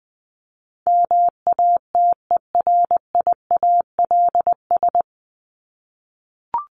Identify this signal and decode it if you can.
Morse code, heard as a keyed tone, about 20 wpm, 700 Hz. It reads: MATERIALS
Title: ES:S Suffix - Top 100 Words with Base Words 20wpm